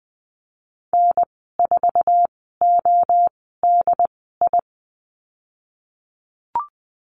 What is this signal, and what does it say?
Morse code, heard as a keyed tone, about 20 wpm, spaced as for 15 wpm, 700 Hz. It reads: N4ODI
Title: 076 Lesson 13g - Letter L - Call Signs 20wpm